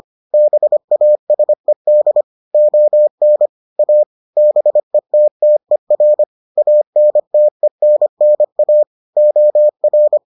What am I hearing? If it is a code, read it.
Morse code, heard as a keyed tone, about 25 wpm, 600 Hz. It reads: BASED ON A BETTER ANTENNA OR